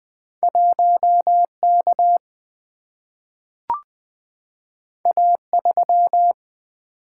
Morse code, heard as a keyed tone, about 20 wpm, 700 Hz. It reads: 1 K A 3
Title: Single-Letter Single-Number 20wpm